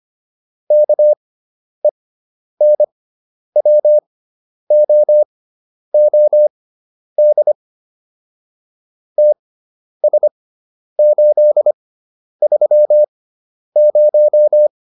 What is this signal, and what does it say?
Morse code, heard as a keyed tone, about 25 wpm, spaced as for 10 wpm, 600 Hz. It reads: KENWOOD TS830